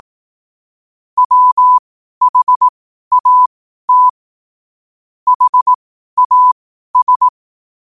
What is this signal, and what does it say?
Morse code, heard as a keyed tone, about 18 wpm, spaced as for 12 wpm, 1000 Hz. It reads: WHAT HAS
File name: qotdfiles / 12wpm